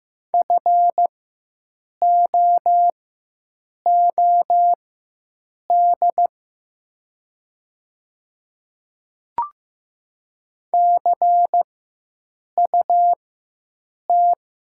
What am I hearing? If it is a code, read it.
Morse code, heard as a keyed tone, about 15 wpm, spaced as for 7 wpm, 700 Hz. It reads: FOOD CUT
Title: 4x Character Spacing - Top 300 Words - 15wpm